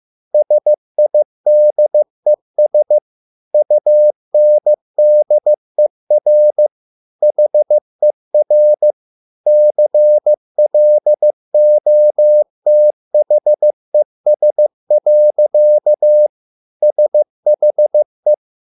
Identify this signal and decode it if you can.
Morse code, heard as a keyed tone, about 15 wpm, 600 Hz. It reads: SIDES UNDER HER CLOTHES. SHE